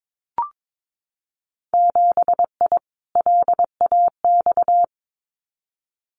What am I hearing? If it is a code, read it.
Morse code, heard as a keyed tone, about 22 wpm, 700 Hz. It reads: 7I LAX
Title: Winter Field Day Exchanges Encore 22wpm